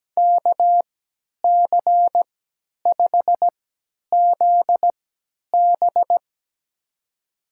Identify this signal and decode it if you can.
Morse code, heard as a keyed tone, about 17 wpm, spaced as for 10 wpm, 700 Hz. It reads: KC5ZB
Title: USA Call Signs - 5 Characters - 3x Character Spacing - 17wpm